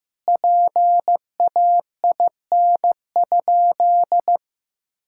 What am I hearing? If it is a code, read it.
Morse code, heard as a keyed tone, about 15 wpm, 700 Hz. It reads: PAIN?
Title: Sentences from Top 1600 Words 15wpm